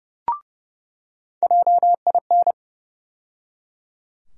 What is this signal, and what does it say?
Morse code, heard as a keyed tone, about 30 wpm, 700 Hz. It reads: JIN
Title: USA Call Sign Postfixes - 3 Letters 30wpm